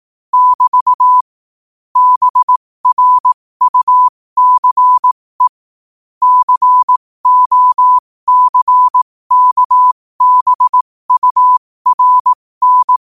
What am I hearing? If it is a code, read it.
Morse code, heard as a keyed tone, about 18 wpm, spaced as for 15 wpm, 1000 Hz. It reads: <BT> BRUCE COCKBURN
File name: qotdfiles / 15wpm